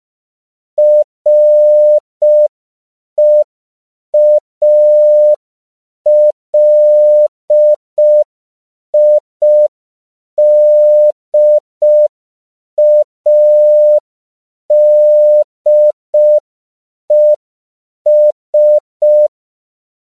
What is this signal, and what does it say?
Morse code, heard as a keyed tone, about 5 wpm, 600 Hz. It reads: REALIDADES
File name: 5PPM